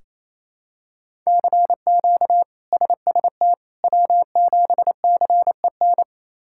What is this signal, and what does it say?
Morse code, heard as a keyed tone, about 28 wpm, 700 Hz. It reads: CQ SST W7CEN
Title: K1USN SST QSO Practice - 28wpm